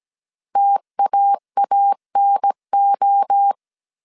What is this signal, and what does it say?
Morse code, heard as a keyed tone, about 17 wpm, 800 Hz. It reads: TAANO